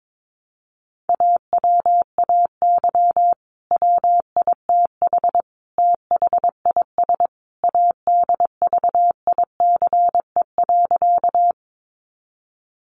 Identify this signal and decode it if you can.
Morse code, heard as a keyed tone, about 22 wpm, 700 Hz. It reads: AWAY WITH THIS ADVICE.
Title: Sentences from Top 1100 Words 22wpm